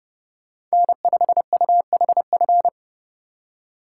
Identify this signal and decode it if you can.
Morse code, heard as a keyed tone, about 35 wpm, 700 Hz. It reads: N5UHF
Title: Speed Racing - USA 1x3 Call Signs 30wpm